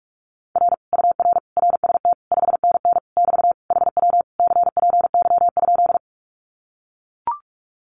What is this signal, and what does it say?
Morse code, heard as a keyed tone, about 44 wpm, 700 Hz. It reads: R UR RST 5NN <BT> HW CPY?
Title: Common QSO Phrases Encore 45wpm